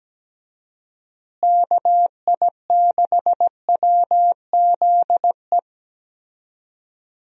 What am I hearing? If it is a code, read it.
Morse code, heard as a keyed tone, about 17 wpm, 700 Hz. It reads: KI6WZE
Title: USA Call Signs - 6 Characters - Repeated 2x - 17wpm